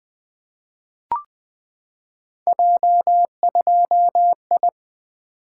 Morse code, heard as a keyed tone, about 20 wpm, 700 Hz. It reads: J2I
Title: Three Letter-Number Combinations - 20wpm